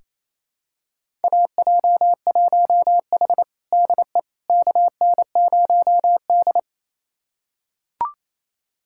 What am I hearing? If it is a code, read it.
Morse code, heard as a keyed tone, about 27 wpm, 700 Hz. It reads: AJ1H DE KN0D
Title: USA Callsign DE Callsign - 4 Characters - 28wpm